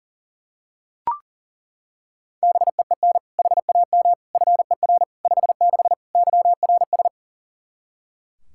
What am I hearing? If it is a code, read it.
Morse code, heard as a keyed tone, about 40 wpm, 700 Hz. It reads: BEEN HAM FER 56 YRS
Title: Practice QSO - Exchange 3 - More Info - 40wpm